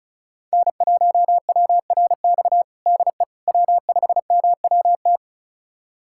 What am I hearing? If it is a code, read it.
Morse code, heard as a keyed tone, about 34 wpm, 700 Hz. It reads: N1WRX DE W5MWT
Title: USA Callsign DE Callsign - 5 Characters - 35wpm